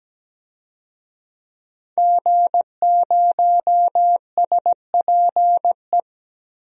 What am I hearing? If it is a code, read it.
Morse code, heard as a keyed tone, about 17 wpm, 700 Hz. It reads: G0SPE
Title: DXCC-100 Call Signs - 5 Characters - Repeated 3x - 17wpm